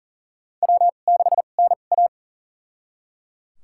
Morse code, heard as a keyed tone, about 39 wpm, spaced as for 29 wpm, 700 Hz. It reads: W6NA
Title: USA Call Signs - 4 Characters - 2x Character Spacing - 40wpm